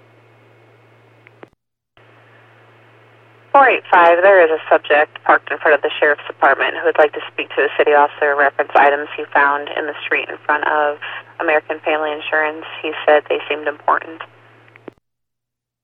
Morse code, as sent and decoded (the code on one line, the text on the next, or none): none